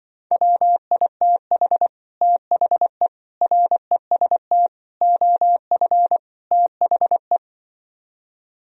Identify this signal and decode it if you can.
Morse code, heard as a keyed tone, about 25 wpm, 700 Hz. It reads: WITH THE REST OF THE
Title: Speed-Racing - Sets of 5 Words 20wpm